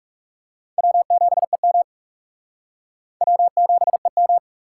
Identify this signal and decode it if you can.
Morse code, heard as a keyed tone, about 37 wpm, 700 Hz. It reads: W7EM W7EM
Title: Speed Racing - USA 1x2 Call Signs 30wpm